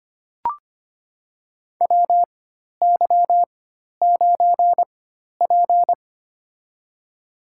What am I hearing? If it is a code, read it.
Morse code, heard as a keyed tone, about 25 wpm, spaced as for 12 wpm, 700 Hz. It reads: WY9P
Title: USA Call Signs - 4 Characters - 4x Character Spacing - 25wpm